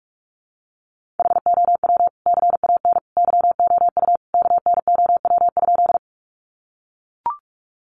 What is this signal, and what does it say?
Morse code, heard as a keyed tone, about 44 wpm, 700 Hz. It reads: HOW CAN YOU KNOW?
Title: Sentences from Top 100 Words 45wpm